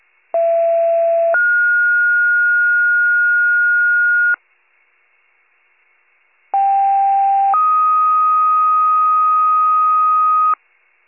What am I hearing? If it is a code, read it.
Two-tone page, heard as a one-second tone, then a three-second tone, about 670 then 1430 Hz, about 790 then 1250 Hz. It is heard from 0.3 to 10.5 s.